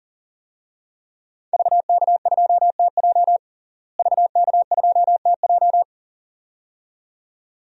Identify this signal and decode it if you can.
Morse code, heard as a keyed tone, about 40 wpm, 700 Hz. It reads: VK2TJ VK2TJ
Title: DXCC-100 Call Signs - 5 Characters - Repeated 2x - 40wpm